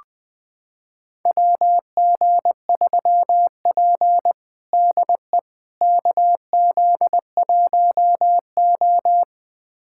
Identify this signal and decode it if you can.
Morse code, heard as a keyed tone, about 20 wpm, 700 Hz. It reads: WG3P DE KZ1O
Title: USA Callsign DE Callsign - 4 Characters - 20wpm